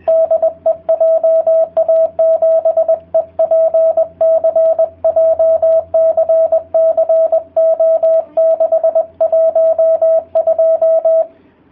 Morse code, heard as a keyed tone, about 20 wpm, 650 Hz, with radio noise. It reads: DE JA7EPC JCC O612